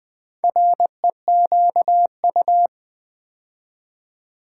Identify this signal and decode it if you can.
Morse code, heard as a keyed tone, about 20 wpm, 700 Hz. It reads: REQU
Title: Build A Word - 4 Letters At A Time - 11-Letter Words - 20wpm